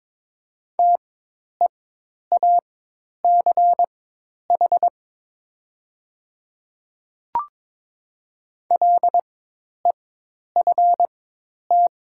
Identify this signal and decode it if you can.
Morse code, heard as a keyed tone, about 22 wpm, spaced as for 10 wpm, 700 Hz. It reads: TEACH LEFT
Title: 4x Character Spacing - Top 500 Words - 22wpm